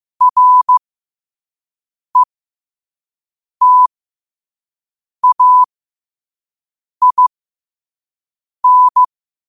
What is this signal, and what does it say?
Morse code, heard as a keyed tone, about 15 wpm, spaced as for 5 wpm, 1000 Hz. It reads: RETAIN
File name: qotdfiles / 5wpm